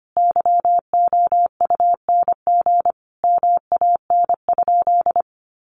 Morse code, heard as a keyed tone, about 25 wpm, 700 Hz. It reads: YOUNG MAN?
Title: Sentences from Top 400 Words 25wpm